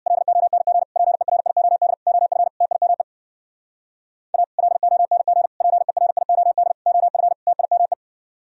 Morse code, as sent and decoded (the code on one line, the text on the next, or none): .-.. --- -. --. / .--. .-. .. --- -.. / --- ..-. / - .. -- . / .- / .-.. --- -. --. / .--. .-. .. --- -.. / --- ..-. / - .. -- .
LONG PERIOD OF TIME A LONG PERIOD OF TIME